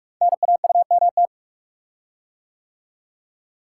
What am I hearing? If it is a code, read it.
Morse code, heard as a keyed tone, about 44 wpm, 690 Hz. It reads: NAUMT
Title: Code Groups - 5 Letters - 45wpm